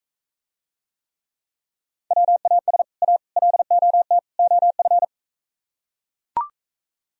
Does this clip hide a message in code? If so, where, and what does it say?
Yes: Morse code from 2.1 to 5.0 s, heard as a keyed tone, about 43 wpm, 700 Hz. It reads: WAS A LOT OF